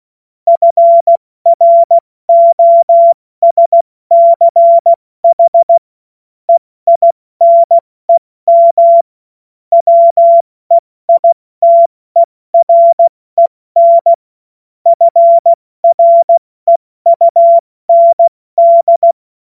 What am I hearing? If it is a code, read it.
Morse code, heard as a keyed tone, about 16 wpm, 680 Hz. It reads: FROSCH EINEM WEITEREN FREUND
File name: Morsekurs / Witze / Witz 08. November 2022 70 F80